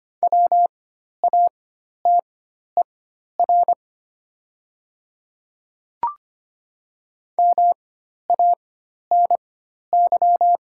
Morse code, heard as a keyed tone, about 25 wpm, spaced as for 12 wpm, 700 Hz. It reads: WATER MANY
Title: 4x Character Spacing - Top 100 Words - 25wpm